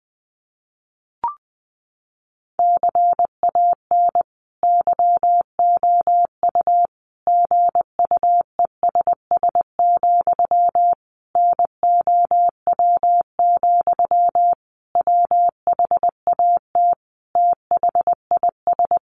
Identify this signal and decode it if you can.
Morse code, heard as a keyed tone, about 20 wpm, 700 Hz. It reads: CAN YOU GUESS, NOW, WHAT THIS
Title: Sentences from Top 1500 Words 20wpm